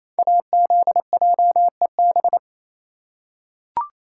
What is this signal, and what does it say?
Morse code, heard as a keyed tone, about 28 wpm, 700 Hz. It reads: AZJEB
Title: Code Groups - 5 Letter:Numbers - 28wpm